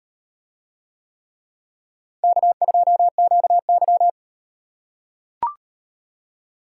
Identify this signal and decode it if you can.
Morse code, heard as a keyed tone, about 38 wpm, 700 Hz. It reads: K2QY